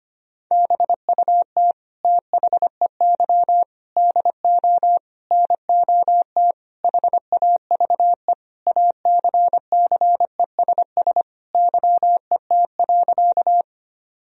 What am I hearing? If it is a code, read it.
Morse code, heard as a keyed tone, about 25 wpm, 700 Hz. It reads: BUT THEY DO NOT HAVE ACCESS YET.
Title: Sentences from Top 1000 Words - Review Part 4 - 25wpm